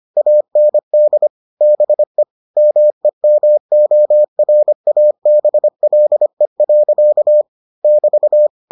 Morse code, heard as a keyed tone, about 25 wpm, 600 Hz. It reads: AND BE MEMORABLE. <BT>